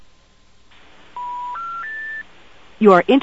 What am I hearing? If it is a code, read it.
Special information tone, with speech.
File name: attintl